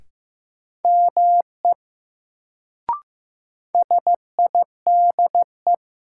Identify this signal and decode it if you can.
Morse code, heard as a keyed tone, about 15 wpm, 700 Hz. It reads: ME SIDE
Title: Sets of 2 Words - New Word Drill - 15wpm